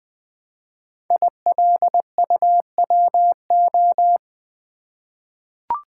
Morse code, heard as a keyed tone, about 20 wpm, 700 Hz. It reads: ILUWO
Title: Code Groups - 5 Letters - 20wpm